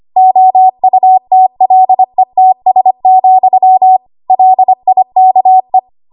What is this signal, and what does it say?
Morse code, heard as a keyed tone, about 25 wpm, 750 Hz. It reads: OUTLETS, LIKE
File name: Morse / 25WPM